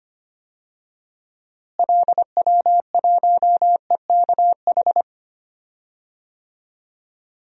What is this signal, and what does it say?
Morse code, heard as a keyed tone, about 25 wpm, 700 Hz. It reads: LW1EKH
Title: DXCC-100 Call Signs - 6 Characters - Repeated 2x - 25wpm